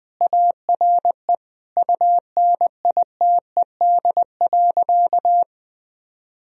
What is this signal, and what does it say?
Morse code, heard as a keyed tone, about 20 wpm, 700 Hz. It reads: ARE UNITED.
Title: Sentences from Top 1000 Words - Review Part 4 - 20wpm